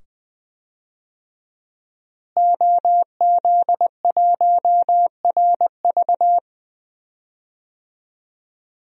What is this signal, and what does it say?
Morse code, heard as a keyed tone, about 20 wpm, 700 Hz. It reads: OZ1RV
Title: DXCC-100 Call Signs - 5 Characters - Repeated 2x - 20wpm